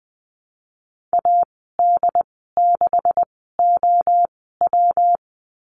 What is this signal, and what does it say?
Morse code, heard as a keyed tone, about 20 wpm, spaced as for 15 wpm, 700 Hz. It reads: AD6OW
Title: 128 Lesson 21g - Number 6 - Call Signs 20wpm